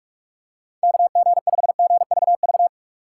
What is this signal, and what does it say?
Morse code, heard as a keyed tone, about 44 wpm, 700 Hz. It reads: KG5GVV